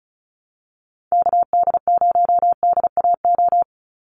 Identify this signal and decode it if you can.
Morse code, heard as a keyed tone, about 34 wpm, 700 Hz. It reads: KD0DAO
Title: USA Call Signs 35wpm